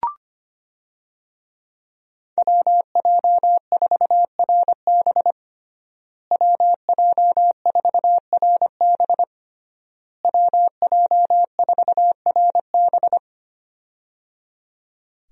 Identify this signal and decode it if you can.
Morse code, heard as a keyed tone, about 25 wpm, 700 Hz. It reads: WJ4RB WJ4RB WJ4RB
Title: USA Call Signs - 5 Characters - Repeated 3x - 25wpm